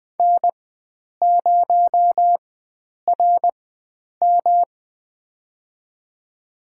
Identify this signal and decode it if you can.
Morse code, heard as a keyed tone, about 20 wpm, spaced as for 9 wpm, 700 Hz. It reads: N0RM
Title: USA Call Signs - 4x Character Spacing - 20wpm